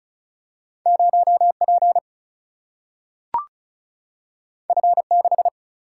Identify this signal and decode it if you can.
Morse code, heard as a keyed tone, about 36 wpm, 700 Hz. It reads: 0 P F 6
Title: Single-Letter Single-Number 35wpm